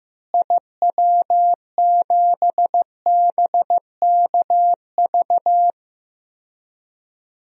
Morse code, heard as a keyed tone, about 15 wpm, 700 Hz. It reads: IW7BKV
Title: DXCC-100 Call Signs - 6 Characters - Repeated 3x - 15wpm